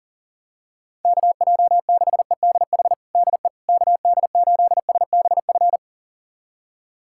Morse code, heard as a keyed tone, about 40 wpm, 700 Hz. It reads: KJ6EDH DE KD8SBF
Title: USA Callsign DE Callsign - 6 Characters - 40wpm